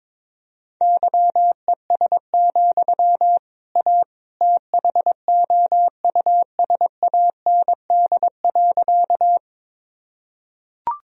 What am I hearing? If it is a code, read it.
Morse code, heard as a keyed tone, about 22 wpm, 700 Hz. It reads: YES, A THOUSAND.
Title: Sentences from Top 1000 Words - Review Part 3 - 22wpm